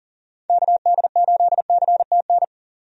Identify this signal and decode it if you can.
Morse code, heard as a keyed tone, about 40 wpm, 700 Hz. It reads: KD8CTN
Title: USA Call Signs - 6 Characters - 40wpm